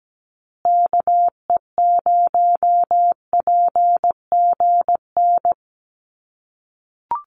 Morse code, heard as a keyed tone, about 17 wpm, 700 Hz. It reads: KE0PGN
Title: USA Call Signs Repeated 3x 17wpm